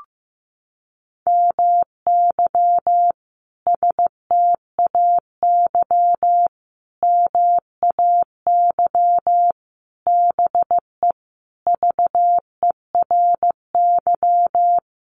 Morse code, heard as a keyed tone, about 15 wpm, 700 Hz. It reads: MY STAY MAY BE VERY
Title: Sentences from Top 1000 Words 15wpm